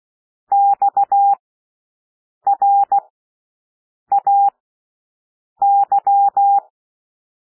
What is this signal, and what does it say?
Morse code, heard as a keyed tone, about 16 wpm, spaced as for 6 wpm, 800 Hz. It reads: XRAY
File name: cwmp3 / W1 SOUND WORDS 16 wpm